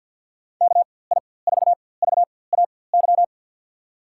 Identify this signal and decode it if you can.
Morse code, heard as a keyed tone, about 49 wpm, spaced as for 23 wpm, 700 Hz. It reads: KI4VAY
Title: USA Call Signs - 6 Characters - 4x Character Spacing - 50wpm